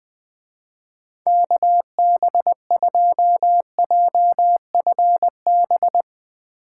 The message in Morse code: KB2JFB

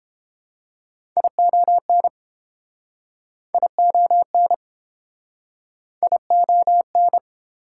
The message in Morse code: ION ION ION

